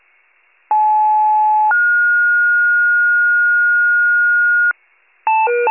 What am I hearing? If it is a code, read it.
Two-tone page, heard as a one-second tone, then a three-second tone, about 860 then 1430 Hz, with noise. It is heard from 0.7 to 4.7 s.